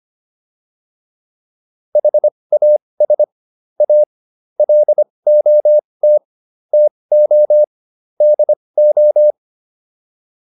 Morse code, heard as a keyed tone, about 25 wpm, spaced as for 20 wpm, 600 Hz. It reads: HAS A LOT TO DO